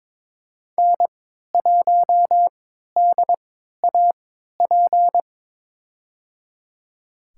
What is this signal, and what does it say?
Morse code, heard as a keyed tone, about 22 wpm, spaced as for 12 wpm, 700 Hz. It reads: N1DAP
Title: USA Call Signs - 5 Characters - 3x Character Spacing - 22wpm